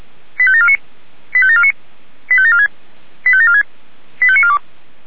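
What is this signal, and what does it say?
Five-tone selective call (ZVEI), heard as quick runs of five tones, tones of about 70 ms. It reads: 85749 85749 85746 85746 85942
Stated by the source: ZVEI five-tone selective call, received by SDR.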